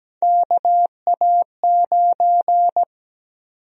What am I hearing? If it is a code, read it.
Morse code, heard as a keyed tone, about 17 wpm, 700 Hz. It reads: KA9
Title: USA Call Sign Prefixes - 17wpm